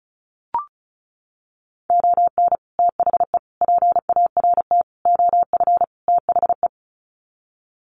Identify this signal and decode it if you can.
Morse code, heard as a keyed tone, about 34 wpm, 700 Hz. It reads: ON THE PART OF THE
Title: Sets of 5 Words 35wpm